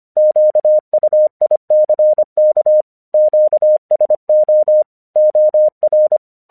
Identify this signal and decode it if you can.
Morse code, heard as a keyed tone, about 25 wpm, 610 Hz. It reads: QUICK QSO OR